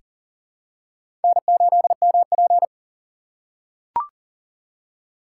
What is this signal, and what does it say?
Morse code, heard as a keyed tone, about 40 wpm, 700 Hz. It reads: N8MP
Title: USA Call Signs - 4 Characters - 4x Character Spacing - 40wpm